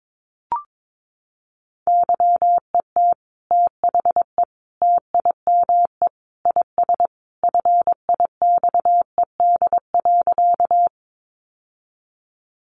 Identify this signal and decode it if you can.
Morse code, heard as a keyed tone, about 22 wpm, 700 Hz. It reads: YET THE TIME IS FIXED.
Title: Sentences from Top 1100 Words 22wpm